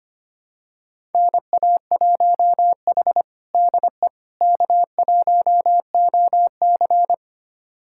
Morse code, heard as a keyed tone, about 25 wpm, 700 Hz. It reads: NA1H DE K1OC
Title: USA Callsign DE Callsign - 4 Characters - 25wpm